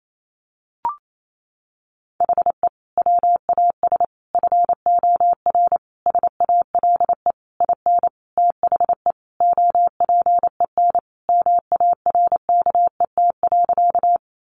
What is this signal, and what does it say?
Morse code, heard as a keyed tone, about 28 wpm, 700 Hz. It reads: HE WAS FOR SALE IN THE OPEN MARKET.
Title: Sentences from Top 1400 Words 28wpm